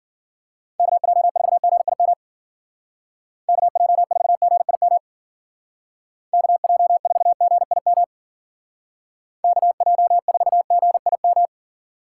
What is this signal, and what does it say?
Morse code, heard as a keyed tone, about 48 wpm, 700 Hz. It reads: KJ4GIM KJ4GIM KJ4GIM KJ4GIM
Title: Speed Racing - USA Call Signs 40wpm